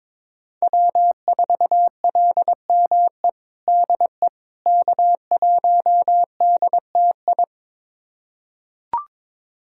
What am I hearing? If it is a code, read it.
Morse code, heard as a keyed tone, about 21 wpm, 700 Hz. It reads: W4LME DE K1DTI
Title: USA Callsign DE Callsign - 5 Characters - 22wpm